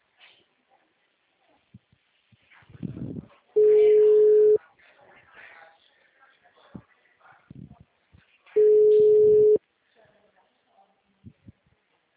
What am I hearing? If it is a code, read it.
Ringback tone (Europe, 425 Hz), with speech.
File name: out- -2012- - - .66374